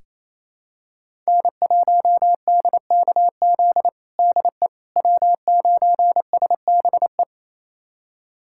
Morse code, heard as a keyed tone, about 28 wpm, 700 Hz. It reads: N1DKZ DE W9SBE
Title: USA Callsign DE Callsign - 5 Characters - 28wpm